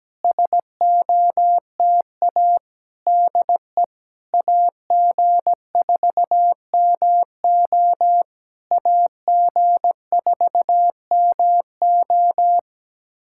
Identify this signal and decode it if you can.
Morse code, heard as a keyed tone, about 17 wpm, 700 Hz. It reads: SOTA DE AG4MO AG4MO
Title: SOTA QSO Practice 17wpm